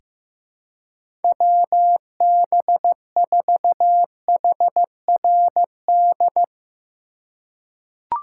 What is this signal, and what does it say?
Morse code, heard as a keyed tone, about 15 wpm, 700 Hz. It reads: WB4HRD
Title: USA Call Signs - 2x Character Spacing - 15wpm